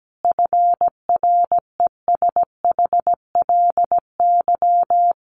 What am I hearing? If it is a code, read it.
Morse code, heard as a keyed tone, about 17 wpm, 700 Hz. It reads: FRESHLY
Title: Sets of 5 Words Encore 17wpm